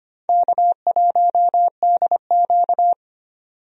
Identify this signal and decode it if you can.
Morse code, heard as a keyed tone, about 25 wpm, 700 Hz. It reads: K1DQ
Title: USA Call Signs - 4 Characters - 25wpm